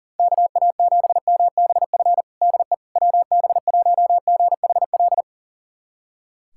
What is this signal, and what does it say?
Morse code, heard as a keyed tone, about 40 wpm, 700 Hz. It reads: KA7MBF DE WB1GHL
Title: USA Callsign DE Callsign - 6 Characters - 40wpm